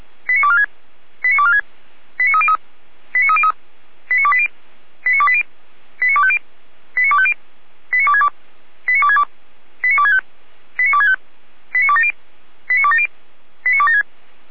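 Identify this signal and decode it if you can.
Five-tone selective call (ZVEI), heard as quick runs of five tones, tones of about 70 ms. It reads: 89246 89246 89303 89303 89280 89280 89250 89250 89262 89262 89265 89265 89270 89270 89276
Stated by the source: ZVEI five-tone selective call received by SDR, from the audio server of Feuerwehr Orlamünde (fire brigade).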